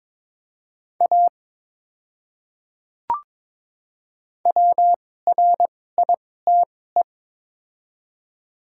Morse code, heard as a keyed tone, about 22 wpm, spaced as for 16 wpm, 700 Hz. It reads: A WRITE
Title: 2x Character Spacing - Top 300 Words - 22wpm